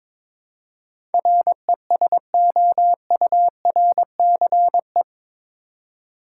Morse code, heard as a keyed tone, about 22 wpm, 700 Hz. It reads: RESOURCE
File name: practice / 22wpm / Eight Letter Words - 22wpm